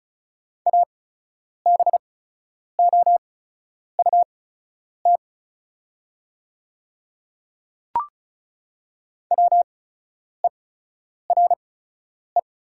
Morse code, heard as a keyed tone, about 36 wpm, spaced as for 10 wpm, 700 Hz. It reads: ABOUT WERE
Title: 8x Character Spacing - Top 100 Words - 35wpm